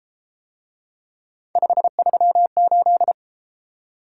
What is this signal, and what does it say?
Morse code, heard as a keyed tone, about 30 wpm, 700 Hz. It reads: 538